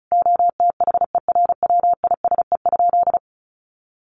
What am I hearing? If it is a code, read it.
Morse code, heard as a keyed tone, about 34 wpm, 700 Hz. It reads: OTHERWISE?
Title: Sentences from Top 1200 Words 35wpm